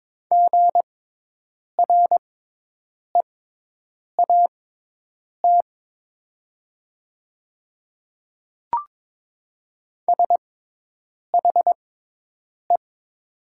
Morse code, heard as a keyed tone, about 22 wpm, spaced as for 8 wpm, 700 Hz. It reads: GREAT SHE
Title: 6x Character Spacing - Top 300 Words - 22wpm